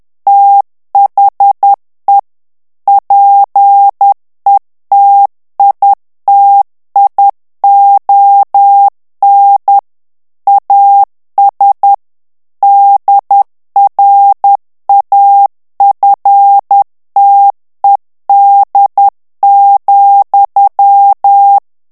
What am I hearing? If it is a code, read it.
Morse code, heard as a keyed tone, about 10 wpm, 800 Hz. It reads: THE PETITION AS DRAFTED,
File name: ANGLAIS texte clair 3 vitesse 10 mots minute